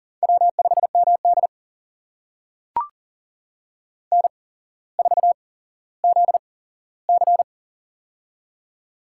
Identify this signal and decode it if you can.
Morse code, heard as a keyed tone, about 42 wpm, spaced as for 17 wpm, 700 Hz. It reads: W5MD N4ZC
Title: USA Call Signs - 8x Character Spacing - 40wpm